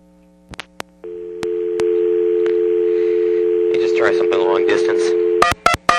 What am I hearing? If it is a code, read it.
Dial tone (North America), 350 + 440 Hz, with speech.
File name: TurtleCreekEWSD